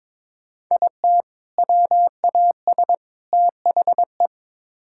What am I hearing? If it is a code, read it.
Morse code, heard as a keyed tone, about 22 wpm, 700 Hz. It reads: IT WAS THE
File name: practice / 22wpm / Build A Phrase - 5-Word Phrases - 22wpm